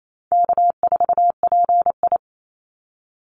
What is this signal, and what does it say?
Morse code, heard as a keyed tone, about 28 wpm, 700 Hz. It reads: K4PI
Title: Easy Call Signs Repeated 3x - 28wpm